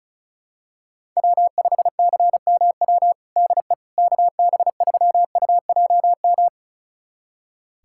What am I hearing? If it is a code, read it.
Morse code, heard as a keyed tone, about 36 wpm, 700 Hz. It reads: W5CMW DE KB3UJM
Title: USA Callsign DE Callsign 35wpm